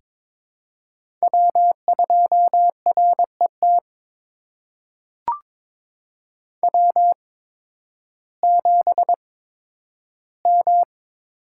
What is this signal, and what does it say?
Morse code, heard as a keyed tone, about 22 wpm, spaced as for 12 wpm, 700 Hz. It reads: W2RET W7M
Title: USA Call Signs - 5 Characters - 8x Character Spacing - 22wpm